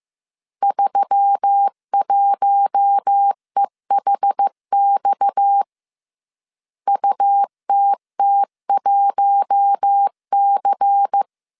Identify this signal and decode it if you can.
Morse code, heard as a keyed tone, about 15 wpm, 800 Hz. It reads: 31EHX UTT1C
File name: CW / DET / 1 9 1500C S A 14wpm 3222